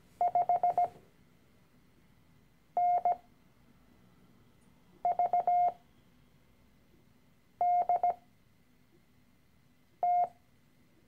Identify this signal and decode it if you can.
Morse code, heard as a keyed tone, about 17 wpm, spaced as for 4 wpm, 710 Hz. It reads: 5NVDT
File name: CW-Online-Kurs / Anfänger / Lektion 013